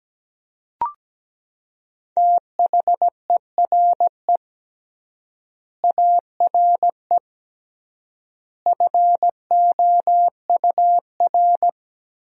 Morse code, heard as a keyed tone, about 17 wpm, 700 Hz. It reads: THERE ARE FOUR
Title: 3x Word-Spacing - Sentences from Top 400 Words - 17wpm